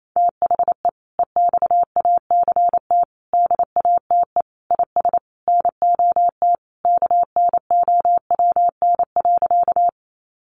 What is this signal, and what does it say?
Morse code, heard as a keyed tone, about 28 wpm, 700 Hz. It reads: THE EXACT DATE IS NOT KNOWN.
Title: Sentences from Top 1300 Words 28wpm